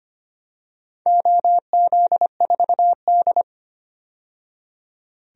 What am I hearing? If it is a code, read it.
Morse code, heard as a keyed tone, about 25 wpm, 700 Hz. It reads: OZ4D
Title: DXCC-100 Call Signs - 4 Characters - 25wpm